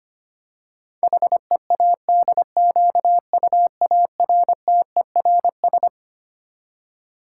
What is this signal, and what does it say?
Morse code, heard as a keyed tone, about 25 wpm, 700 Hz. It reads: HEADQUARTERS
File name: Twelve Letter Words - 25wpm